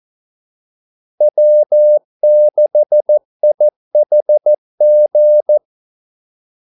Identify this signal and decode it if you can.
Morse code, heard as a keyed tone, about 14 wpm, 600 Hz. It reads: W6IHG
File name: Callsigns-14wpm